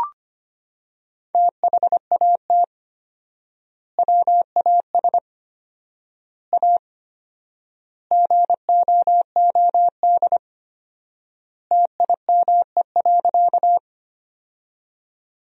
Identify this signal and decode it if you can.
Morse code, heard as a keyed tone, about 25 wpm, 700 Hz. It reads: THAT WAS A GOOD TIME.
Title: 4x Word-Spacing - Sentences from Top 200 Words - 25wpm